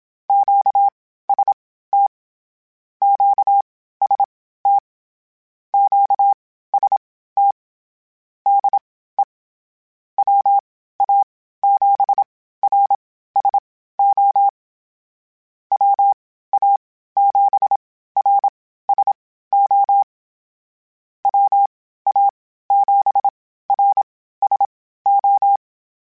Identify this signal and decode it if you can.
Morse code, heard as a keyed tone, about 27 wpm, spaced as for 15 wpm, 800 Hz. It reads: QST QST QST DE WA7RSO WA7RSO WA7RSO
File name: Misc / Misc-mp3s / QST QST @25